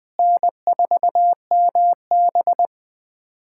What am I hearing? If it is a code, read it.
Morse code, heard as a keyed tone, about 20 wpm, 700 Hz. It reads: N4MB
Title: USA Callsign DE Callsign - 4 Characters - 20wpm